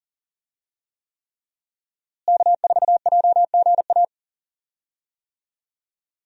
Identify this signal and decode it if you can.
Morse code, heard as a keyed tone, about 38 wpm, 700 Hz. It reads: K4JGA
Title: USA Call Signs - 5 Characters - Repeated 3x - 40wpm